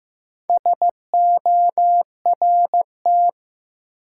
Morse code, heard as a keyed tone, about 15 wpm, 700 Hz. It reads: SORT
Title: Sets of 2 Words - New Word Drill - 15wpm